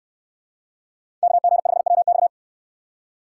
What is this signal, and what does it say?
Morse code, heard as a keyed tone, about 69 wpm, 700 Hz. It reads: KG5WX